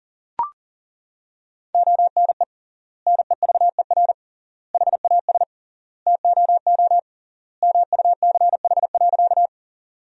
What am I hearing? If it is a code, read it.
Morse code, heard as a keyed tone, about 40 wpm, 700 Hz. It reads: ONE NEVER HAS TOO MUCH.
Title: 3x Word-Spacing - Sentences from Top 300 Words - 40wpm